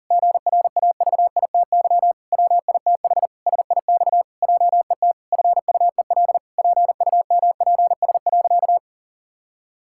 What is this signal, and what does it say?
Morse code, heard as a keyed tone, about 40 wpm, 700 Hz. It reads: GRAVITY WITH SIX JET FUEL PUMPS.
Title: Pangram Challenge v2 - 40wpm